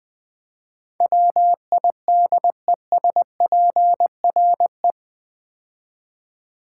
Morse code, heard as a keyed tone, about 20 wpm, 700 Hz. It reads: WIDESPRE
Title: Build A Word - 4 Letters At A Time - 10-Letter Words - 20wpm